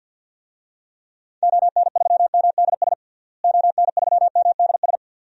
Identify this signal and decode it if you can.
Morse code, heard as a keyed tone, about 49 wpm, 700 Hz. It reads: ON3MDS ON3MDS
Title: DXCC-100 Call Signs - 6 Characters - Repeated 2x - 50wpm